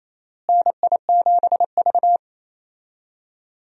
Morse code, heard as a keyed tone, about 27 wpm, 700 Hz. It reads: NI7V
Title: USA Call Signs - 4 Characters - Repeated 3x - 28wpm